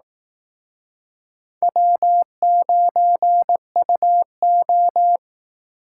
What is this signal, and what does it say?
Morse code, heard as a keyed tone, about 18 wpm, 700 Hz. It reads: W9UO
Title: Speed Racing - USA Call Signs 15wpm